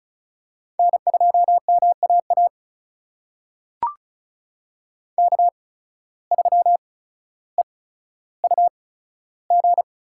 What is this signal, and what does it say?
Morse code, heard as a keyed tone, about 36 wpm, spaced as for 15 wpm, 700 Hz. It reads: N2MAA K3EUG
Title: USA Call Signs - 5 Characters - 8x Character Spacing - 35wpm